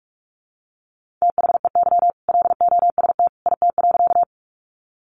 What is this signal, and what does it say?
Morse code, heard as a keyed tone, about 46 wpm, 700 Hz. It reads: THEY LOST IT.
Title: Sentences from Top 1400 Words 45wpm